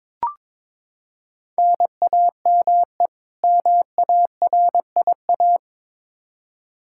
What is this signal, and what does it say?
Morse code, heard as a keyed tone, about 22 wpm, 700 Hz. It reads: NAME MARIA
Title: Practice QSO - Exchange 1 - Beginning - 22wpm